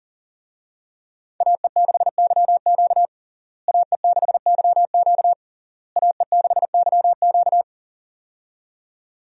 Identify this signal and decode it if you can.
Morse code, heard as a keyed tone, about 40 wpm, 700 Hz. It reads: AE6YQ AE6YQ AE6YQ
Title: USA Call Signs - 5 Characters - Repeated 3x - 40wpm